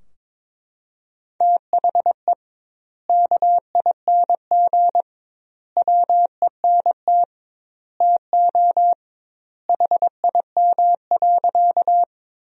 Morse code, heard as a keyed tone, about 22 wpm, 700 Hz. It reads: THE KING WENT TO HIM.
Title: 2x Word-Spacing - Sentences from Top 400 Words - 22wpm